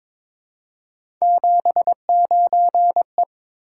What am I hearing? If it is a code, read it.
Morse code, heard as a keyed tone, about 23 wpm, 700 Hz. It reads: 79E
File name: practice / 22wpm / Three Letter-Number Combinations - 22wpm